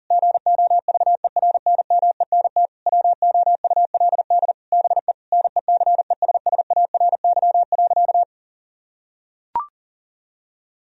Morse code, heard as a keyed tone, about 40 wpm, 700 Hz. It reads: GOVERNMENT WOULD BE NECESSARY.
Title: Sentences from Top 1000 Words - Review Part 3 - 40wpm